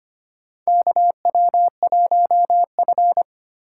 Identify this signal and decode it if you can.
Morse code, heard as a keyed tone, about 25 wpm, 700 Hz. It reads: KW1F
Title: USA Call Signs - 4 Characters - 6x Character Spacing - 25wpm